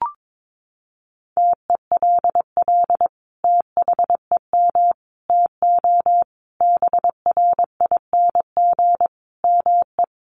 Morse code, heard as a keyed tone, about 22 wpm, 700 Hz. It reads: TELL THEM TO BRING ME
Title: Sentences from Top 1200 Words 22wpm